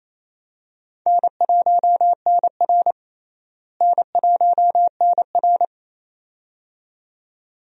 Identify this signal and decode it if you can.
Morse code, heard as a keyed tone, about 28 wpm, 700 Hz. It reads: N1NR N1NR
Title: USA Call Signs - 4 Characters - Repeated 2x - 28wpm